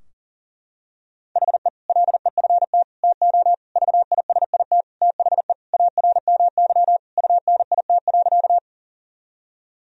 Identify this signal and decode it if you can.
Morse code, heard as a keyed tone, about 40 wpm, 700 Hz. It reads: HE LEFT TO VISIT THE ARMY UNIT.
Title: Sentences from Top 1000 Words - Review Part 2 - 40wpm